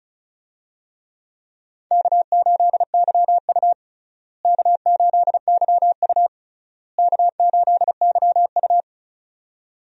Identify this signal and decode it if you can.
Morse code, heard as a keyed tone, about 36 wpm, 700 Hz. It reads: K8YU K8YU K8YU
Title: USA Call Signs - 4 Characters - Repeated 3x - 35wpm